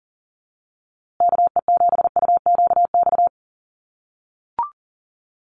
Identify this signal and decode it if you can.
Morse code, heard as a keyed tone, about 40 wpm, 700 Hz. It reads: KE7UQX
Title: USA Call Signs Repeated 2x 40wpm